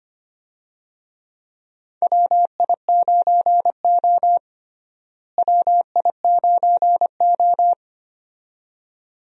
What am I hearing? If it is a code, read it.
Morse code, heard as a keyed tone, about 25 wpm, 700 Hz. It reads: WI9O WI9O
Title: USA Call Signs - 4 Characters - Repeated 2x - 25wpm